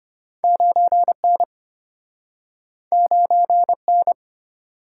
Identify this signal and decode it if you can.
Morse code, heard as a keyed tone, about 33 wpm, 700 Hz. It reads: 9N 9N